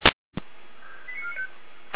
Five-tone selective call (ZVEI), heard as quick runs of five tones, tones of about 80 ms. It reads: 89495